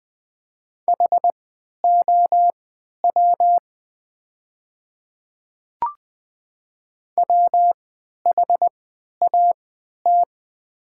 Morse code, heard as a keyed tone, about 20 wpm, spaced as for 11 wpm, 690 Hz. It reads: HOW WHAT
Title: 3x Character Spacing - Top 300 Words - 20wpm